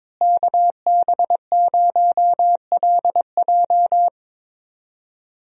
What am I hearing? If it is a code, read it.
Morse code, heard as a keyed tone, about 22 wpm, 700 Hz. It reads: KB0LJ
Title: USA Call Signs - 5 Characters - Repeated 2x - 22wpm